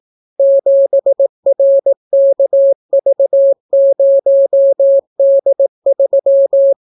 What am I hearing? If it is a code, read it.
Morse code, heard as a keyed tone, about 18 wpm, 550 Hz. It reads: 7RKV0D3